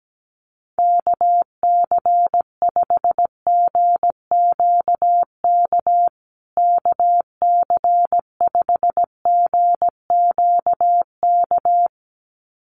Morse code, heard as a keyed tone, about 17 wpm, 700 Hz. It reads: KC5GQK KC5GQK
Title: USA Call Signs Repeated 2x 17wpm